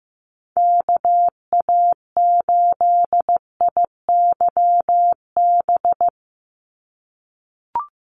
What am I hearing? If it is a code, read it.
Morse code, heard as a keyed tone, about 15 wpm, 700 Hz. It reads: KA8IYB
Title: USA Call Signs 15wpm